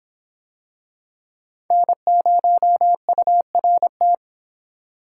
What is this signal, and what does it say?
Morse code, heard as a keyed tone, about 26 wpm, 700 Hz. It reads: N0URT